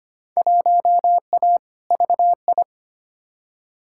Morse code, heard as a keyed tone, about 25 wpm, 700 Hz. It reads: 1A VI
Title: Field Day Exchanges Encore - 25wpm